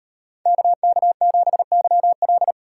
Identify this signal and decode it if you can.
Morse code, heard as a keyed tone, about 38 wpm, 700 Hz. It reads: KK7YL